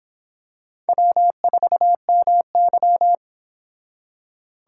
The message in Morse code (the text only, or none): W4MY